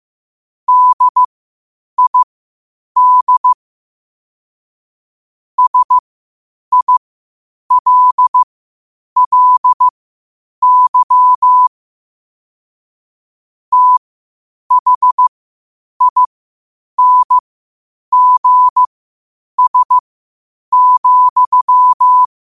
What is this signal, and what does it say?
Morse code, heard as a keyed tone, about 15 wpm, spaced as for 8 wpm, 1000 Hz. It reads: DID SILLY THINGS,
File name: qotdfiles / 8wpm